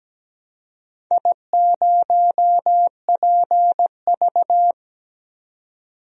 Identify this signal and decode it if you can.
Morse code, heard as a keyed tone, about 17 wpm, 700 Hz. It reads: I0PV
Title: DXCC-100 Call Signs - 4 Characters - 17wpm